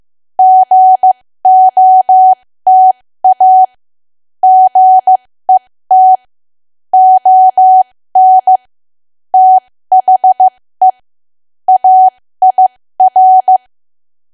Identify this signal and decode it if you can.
Morse code, heard as a keyed tone, about 15 wpm, spaced as for 13 wpm, 750 Hz. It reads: GOTA GET ON THE AIR
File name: cwa / W1AW-Level2 / 13WPM